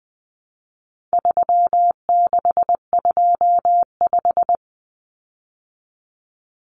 Morse code, heard as a keyed tone, about 20 wpm, 700 Hz. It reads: 3625
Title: Common Cut Numbers - 4-Digit 20wpm